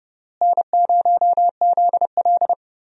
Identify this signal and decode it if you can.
Morse code, heard as a keyed tone, about 32 wpm, 700 Hz. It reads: N0ZL
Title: USA Call Signs - 4 Characters - Repeated 2x - 30wpm